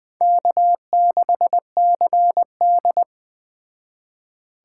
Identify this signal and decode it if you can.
Morse code, heard as a keyed tone, about 20 wpm, 700 Hz. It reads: K6CD
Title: Speed Racing - USA 1x2 Call Signs 17wpm